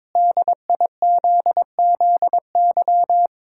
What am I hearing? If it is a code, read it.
Morse code, heard as a keyed tone, about 22 wpm, 700 Hz. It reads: DIZZY
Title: Pangram Challenge v2 - 22wpm